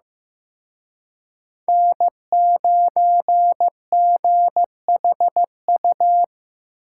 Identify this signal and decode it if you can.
Morse code, heard as a keyed tone, about 18 wpm, 700 Hz. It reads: N9GHU
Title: Speed Racing - USA Call Signs 15wpm